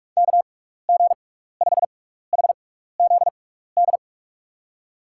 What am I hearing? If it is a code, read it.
Morse code, heard as a keyed tone, about 46 wpm, spaced as for 15 wpm, 700 Hz. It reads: KG5HZD
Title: USA Call Signs - 6 Characters - 6x Character Spacing - 45wpm